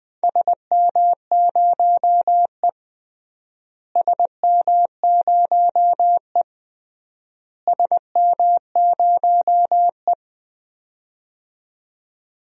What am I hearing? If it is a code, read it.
Morse code, heard as a keyed tone, about 19 wpm, 700 Hz. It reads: SM0E SM0E SM0E
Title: DXCC-100 Call Signs - 4 Characters - Repeated 3x - 20wpm